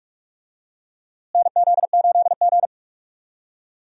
Morse code, heard as a keyed tone, about 43 wpm, 700 Hz. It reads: NZ8G
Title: USA Call Signs - 4 Characters - Repeated 2x - 45wpm